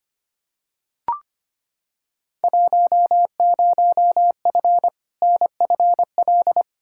Morse code, heard as a keyed tone, about 25 wpm, 700 Hz. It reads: 10F NFL
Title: Field Day Exchanges Encore - 25wpm